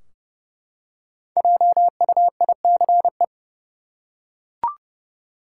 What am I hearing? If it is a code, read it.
Morse code, heard as a keyed tone, about 31 wpm, 700 Hz. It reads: JUICE
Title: Five Letter Words - 30wpm